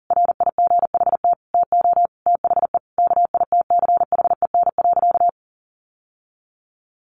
Morse code, heard as a keyed tone, about 40 wpm, 700 Hz. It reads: RIGHT TO THE KITCHEN.
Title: Sentences from Top 1400 Words 40wpm